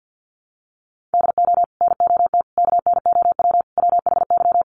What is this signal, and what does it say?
Morse code, heard as a keyed tone, about 56 wpm, 700 Hz. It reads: DO NOT KNOW WHY
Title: Sets of 4 Words 50wpm